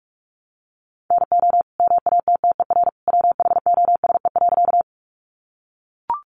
Morse code, heard as a keyed tone, about 44 wpm, 700 Hz. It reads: NO MATTER WHOSE.
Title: Sentences from Top 1200 Words 45wpm